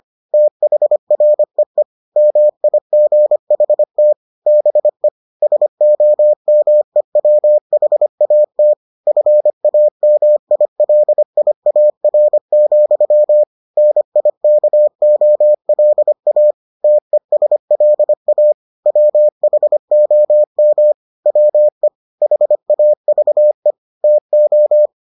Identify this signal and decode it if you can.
Morse code, heard as a keyed tone, about 25 wpm, 600 Hz. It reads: THREE MIGHT BE SOMEWHAT FAMILIAR, NIKOLA TESLA WHOM WE HAVE TO